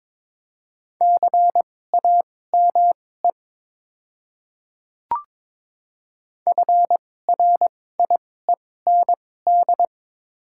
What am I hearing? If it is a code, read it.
Morse code, heard as a keyed tone, about 22 wpm, spaced as for 16 wpm, 690 Hz. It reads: CAME FRIEND
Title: 2x Character Spacing - Top 500 Words - 22wpm